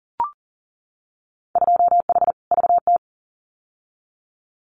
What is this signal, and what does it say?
Morse code, heard as a keyed tone, about 40 wpm, 700 Hz. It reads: 2H VT
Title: Winter Field Day Exchanges 40wpm